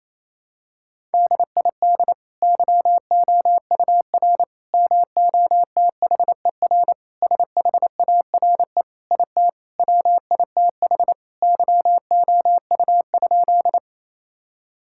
Morse code, heard as a keyed tone, about 27 wpm, 700 Hz. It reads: DID YOUR MOTHER SHARE IT WITH YOU?
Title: Sentences from Top 1000 Words - Review Part 6 - 28wpm